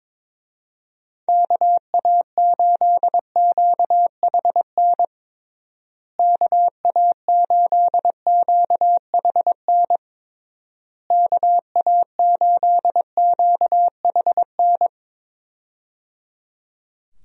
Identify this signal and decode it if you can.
Morse code, heard as a keyed tone, about 22 wpm, 700 Hz. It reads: KA8QHN KA8QHN KA8QHN
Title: USA Call Signs - 6 Characters - Repeated 3x - 22wpm